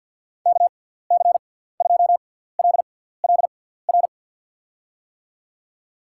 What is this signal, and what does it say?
Morse code, heard as a keyed tone, about 53 wpm, spaced as for 17 wpm, 700 Hz. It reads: KC2LLR
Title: USA Call Signs - 6x Character Spacing - 50wpm